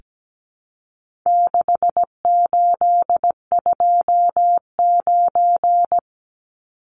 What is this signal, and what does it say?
Morse code, heard as a keyed tone, about 17 wpm, 700 Hz. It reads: 6829
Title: Numbers - 4-Digit 17wpm